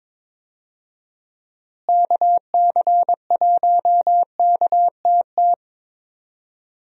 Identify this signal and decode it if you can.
Morse code, heard as a keyed tone, about 22 wpm, 700 Hz. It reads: KC1KTT
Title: USA Call Signs - 6 Characters - Repeated 3x - 22wpm